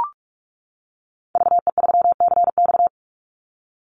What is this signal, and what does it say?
Morse code, heard as a keyed tone, about 46 wpm, 700 Hz. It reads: VE3CX
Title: Call Signs - 45wpm